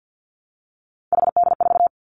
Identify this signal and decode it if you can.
Morse code, heard as a keyed tone, about 50 wpm, 700 Hz. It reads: HD4
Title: Three Challenging Letter-Number Combinations 50wpm